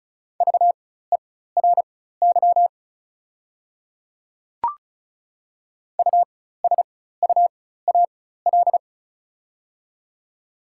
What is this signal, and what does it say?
Morse code, heard as a keyed tone, about 36 wpm, spaced as for 16 wpm, 700 Hz. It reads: VERY USUAL
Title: 4x Character Spacing - Top 500 Words - 35wpm